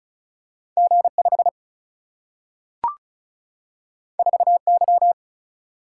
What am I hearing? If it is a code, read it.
Morse code, heard as a keyed tone, about 36 wpm, 700 Hz. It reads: G5 4Y
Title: Single-Letter Single-Number 35wpm